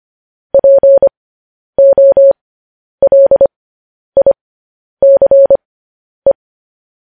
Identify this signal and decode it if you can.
Morse code, heard as a keyed tone, about 25 wpm, spaced as for 10 wpm, 550 Hz. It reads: POLICE
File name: podcast / News headlines-1x-25x10